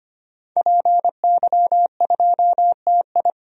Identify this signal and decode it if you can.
Morse code, heard as a keyed tone, about 25 wpm, 700 Hz. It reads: PY2TI
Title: DXCC-100 Call Signs - 5 Characters - 25wpm